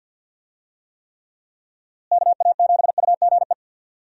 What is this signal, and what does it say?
Morse code, heard as a keyed tone, about 50 wpm, 700 Hz. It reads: KA7UGE
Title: USA Call Signs - 6 Characters - Repeated 3x - 50wpm